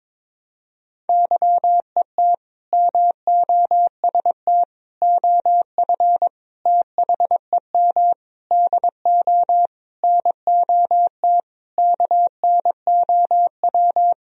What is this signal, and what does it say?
Morse code, heard as a keyed tone, about 22 wpm, 700 Hz. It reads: YET MOST OF THEM DO NOT KNOW
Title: Sentences from Top 1000 Words - Review Part 1 - 22wpm